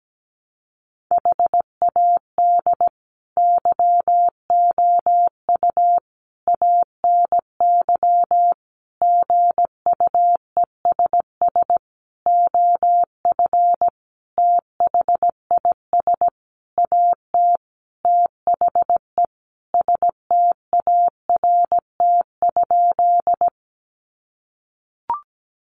Morse code, heard as a keyed tone, about 17 wpm, 700 Hz. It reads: HAD YOU ANY GUESS OF THIS AT THE START?
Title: Sentences from Top 1500 Words 17wpm